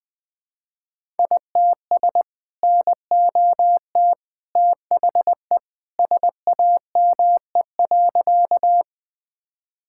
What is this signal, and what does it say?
Morse code, heard as a keyed tone, about 21 wpm, 700 Hz. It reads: ITS NOT THE SAME.
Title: Sentences with 2-Word Phrases - Spoken Beginning - 20wpm